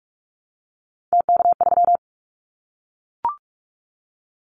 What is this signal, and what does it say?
Morse code, heard as a keyed tone, about 44 wpm, 700 Hz. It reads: TK3
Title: Callsign Prefixes for Top 100 DXCC - 45wpm